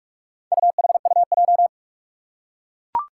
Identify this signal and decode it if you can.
Morse code, heard as a keyed tone, about 41 wpm, 700 Hz. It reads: UHUJ
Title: Four Challenging Letter-Number Combinations 45wpm